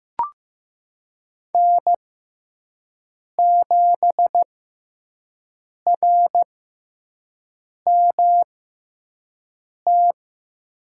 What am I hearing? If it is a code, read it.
Morse code, heard as a keyed tone, about 15 wpm, spaced as for 5 wpm, 700 Hz. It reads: N7RMT
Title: USA Call Signs - 6x Character Spacing - 15wpm